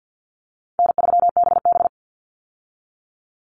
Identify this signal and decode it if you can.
Morse code, heard as a keyed tone, about 50 wpm, 700 Hz. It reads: N3BB
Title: Call Signs Repeated 3x - 50wpm